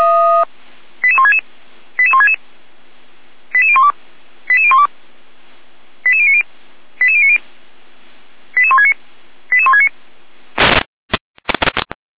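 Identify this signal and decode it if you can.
Five-tone selective call (ZVEI), heard as quick runs of five tones, tones of about 70 ms. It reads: 88166 88166 88012 88012 88090 88090 88169 88169
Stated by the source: ZVEI five-tone selective call 88451 received by SDR.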